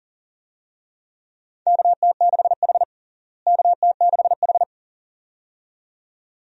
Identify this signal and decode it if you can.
Morse code, heard as a keyed tone, about 40 wpm, 700 Hz. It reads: KT6H KT6H
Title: USA Call Signs - 4 Characters - Repeated 2x - 40wpm